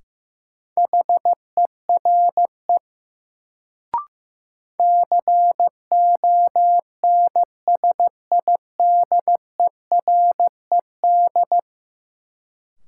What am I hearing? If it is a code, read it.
Morse code, heard as a keyed tone, about 15 wpm, 700 Hz. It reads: HERE CONSIDERED
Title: Sets of 5 Words - New Word Drill - 15wpm